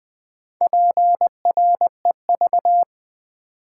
Morse code, heard as a keyed tone, about 20 wpm, 700 Hz. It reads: PREV
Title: Build A Word - 4 Letters At A Time - 10-Letter Words - 20wpm